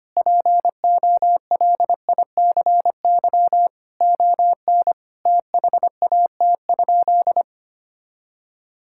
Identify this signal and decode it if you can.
Morse code, heard as a keyed tone, about 25 wpm, 700 Hz. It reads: POLICY ON THAT?
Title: Sentences from Top 1000 Words - Review Part 5 - 25wpm